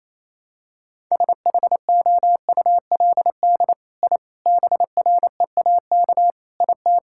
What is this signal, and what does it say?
Morse code, heard as a keyed tone, about 28 wpm, 700 Hz. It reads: SHOULD I BREAK IT
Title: Sentences from Top 1000 Words - Review Part 6 - 28wpm